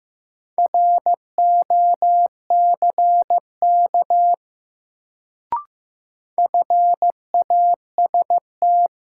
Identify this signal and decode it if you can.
Morse code, heard as a keyed tone, about 15 wpm, 700 Hz. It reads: ROCK FAST
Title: Sentences from Top 400 Words - New Word Drill - 15wpm